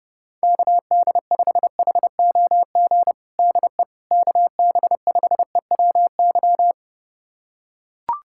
Morse code, heard as a keyed tone, about 28 wpm, 700 Hz. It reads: KD5HOG DE KB5EWY
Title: USA Callsign DE Callsign - 6 Characters - 30wpm